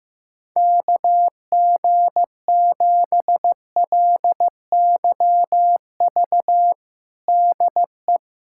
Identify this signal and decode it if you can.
Morse code, heard as a keyed tone, about 15 wpm, 700 Hz. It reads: KG7LYV DE
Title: USA Callsign DE Callsign - 6 Characters - 15wpm